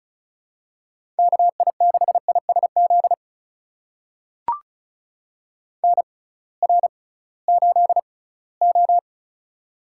Morse code, heard as a keyed tone, about 36 wpm, spaced as for 21 wpm, 700 Hz. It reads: KI6ISZ NR8O
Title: USA Call Signs - 6x Character Spacing - 35wpm